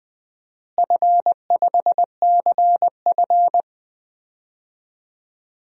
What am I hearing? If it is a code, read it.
Morse code, heard as a keyed tone, about 21 wpm, 700 Hz. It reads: F5CF
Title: DXCC-100 Call Signs - 4 Characters - Repeated 3x - 20wpm